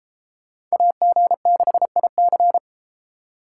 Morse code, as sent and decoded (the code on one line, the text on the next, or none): .- --. -.... .. -.-.
AG6IC